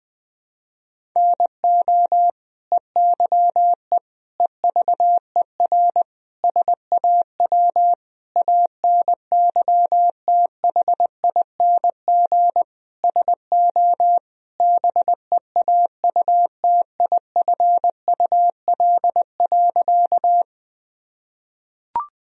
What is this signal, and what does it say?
Morse code, heard as a keyed tone, about 20 wpm, 700 Hz. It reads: NO EYE EVER SAW ANYTHING SO BEAUTIFUL.
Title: Sentences from Top 1000 Words - Review Part 1 - 20wpm